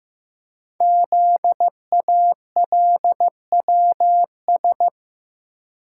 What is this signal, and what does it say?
Morse code, heard as a keyed tone, about 15 wpm, 700 Hz. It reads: ZALWS
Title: Code Groups - 5 Letters - 15wpm